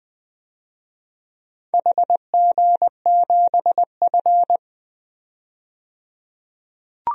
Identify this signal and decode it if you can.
Morse code, heard as a keyed tone, about 20 wpm, 700 Hz. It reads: HG7F
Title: DXCC-100 Call Signs - 4 Characters - 20wpm